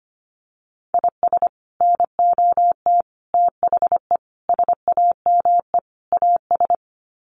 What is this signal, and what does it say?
Morse code, heard as a keyed tone, about 25 wpm, 700 Hz. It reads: IS NOT THE SAME AS
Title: Sets of 5 Words Encore 25wpm